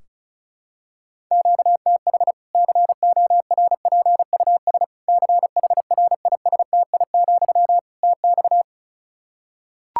Morse code, heard as a keyed tone, about 34 wpm, 700 Hz. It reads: QTH CORPUS CHRISTI, TX
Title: K1USN SST QSO Practice 35wpm